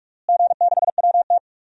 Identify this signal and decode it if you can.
Morse code, heard as a keyed tone, about 45 wpm, 700 Hz. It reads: G6WT
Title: DXCC-100 Call Signs - 4 Characters - Repeated 2x - 45wpm